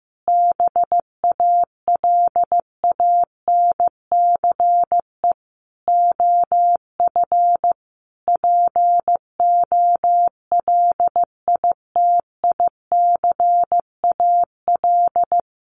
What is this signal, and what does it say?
Morse code, heard as a keyed tone, about 15 wpm, 700 Hz. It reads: BALANCE OF POLITICAL
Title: Sentences from Top 900 Words 15wpm